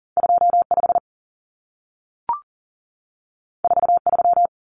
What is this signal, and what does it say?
Morse code, heard as a keyed tone, about 40 wpm, 700 Hz. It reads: 25 43
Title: Common Cut Numbers - 2-Digit 40wpm